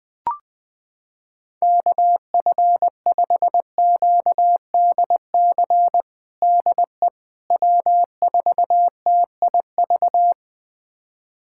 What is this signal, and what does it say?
Morse code, heard as a keyed tone, about 21 wpm, 700 Hz. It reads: KF5QDC DE W4TIV
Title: USA Callsign DE Callsign 20wpm